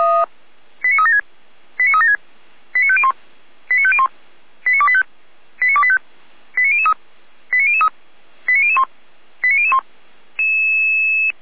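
Five-tone selective call (ZVEI), heard as quick runs of five tones, tones of about 70 ms. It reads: 89376 89376 89501 89501 89285 89285 89003 89003 89001 89001
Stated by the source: ZVEI five-tone selective call received via SDR, from the Feuerwehr Orlamünde audio archive.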